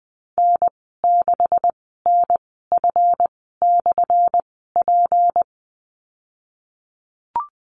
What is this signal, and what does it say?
Morse code, heard as a keyed tone, about 20 wpm, spaced as for 15 wpm, 700 Hz. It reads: N6NF/P
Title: 179 Lesson 29g - Slash - Call Signs 20wpm